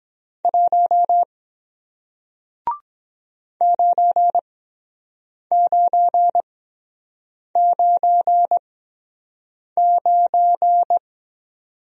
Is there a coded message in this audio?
Yes: Morse code from 0.4 to 11.0 s, heard as a keyed tone, about 23 wpm, spaced as for 7 wpm, 700 Hz. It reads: 1 9999